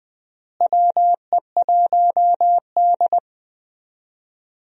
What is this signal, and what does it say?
Morse code, heard as a keyed tone, about 20 wpm, 700 Hz. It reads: WE1D
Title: USA Call Signs - 4 Characters - Repeated 3x - 20wpm